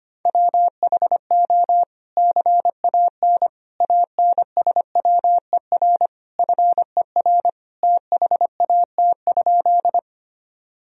Morse code, heard as a keyed tone, about 25 wpm, 700 Hz. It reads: WHO CAN ANSWER FER THAT?
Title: Sentences from Top 300 Words and Abbreviations 25wpm